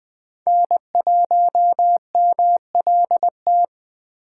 Morse code, heard as a keyed tone, about 20 wpm, 700 Hz. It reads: N1MLT